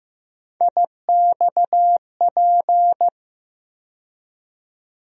Morse code, heard as a keyed tone, about 15 wpm, 700 Hz. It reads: IXP
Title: USA Call Sign Postfixes - 3 Letters 15wpm